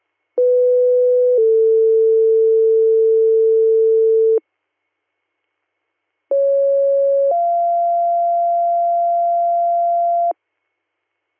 Two-tone page, heard as a one-second tone, then a three-second tone, about 500 then 450 Hz, about 550 then 690 Hz. It is heard from 0.4 to 10.3 s.